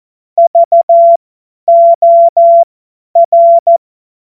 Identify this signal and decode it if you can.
Morse code, heard as a keyed tone, about 14 wpm, spaced as for 10 wpm, 680 Hz. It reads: VOR
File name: Morsekurs / Witze / Witz 18. Januar 2022, 50 F70